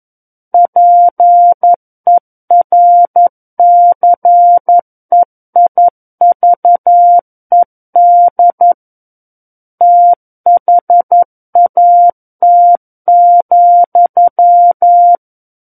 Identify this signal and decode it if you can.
Morse code, heard as a keyed tone, about 11 wpm, 700 Hz. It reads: PERCEIVED THAT,
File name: 11 WPM